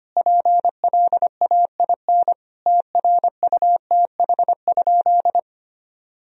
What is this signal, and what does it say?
Morse code, heard as a keyed tone, about 25 wpm, 700 Hz. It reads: PLAIN TRUTH?
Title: Sentences from Top 1000 Words - Review Part 2 - 25wpm